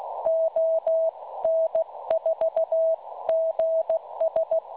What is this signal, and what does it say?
Morse code, heard as a keyed tone, about 15 wpm, 650 Hz. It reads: ON4GS
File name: cwronde / websdr recording .6kHz